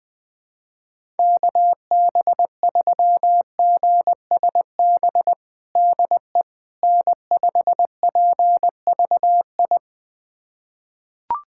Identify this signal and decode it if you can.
Morse code, heard as a keyed tone, about 20 wpm, 700 Hz. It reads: KB3GSB DE N5PVI
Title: USA Callsign DE Callsign 20wpm